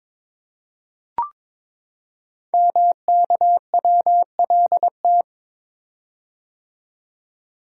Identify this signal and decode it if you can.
Morse code, heard as a keyed tone, about 22 wpm, 700 Hz. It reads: MKWLT
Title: Code Groups - 5 Letter:Numbers - 22wpm